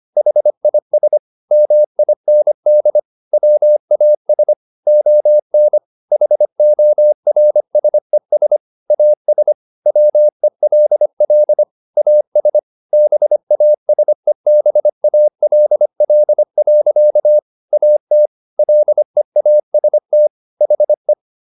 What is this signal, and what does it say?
Morse code, heard as a keyed tone, about 25 wpm, 600 Hz. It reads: HIS MIND WAS ON HORSES AS WELL AS BASEBALL. AT LEAST HE
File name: the old man and the sea part 1 25wpm